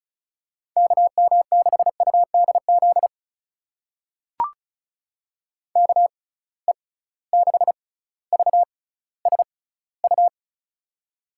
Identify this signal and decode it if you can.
Morse code, heard as a keyed tone, about 34 wpm, spaced as for 18 wpm, 700 Hz. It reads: KM6UDZ KE6VSU
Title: USA Call Signs - 6x Character Spacing - 35wpm